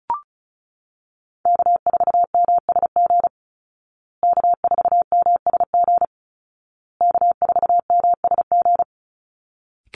Morse code, heard as a keyed tone, about 34 wpm, 700 Hz. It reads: K4MSG K4MSG K4MSG
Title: Call Signs Repeated 2x - 35wpm